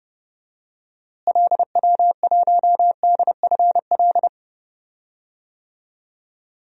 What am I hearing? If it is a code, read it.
Morse code, heard as a keyed tone, about 30 wpm, 700 Hz. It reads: LW1DFL
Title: DXCC-100 Call Signs - 6 Characters - Repeated 3x - 30wpm